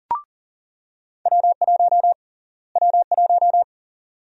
Morse code, heard as a keyed tone, about 40 wpm, 700 Hz. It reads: W1 W1
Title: USA Call Sign Prefixes - Repeated 2x - 40wpm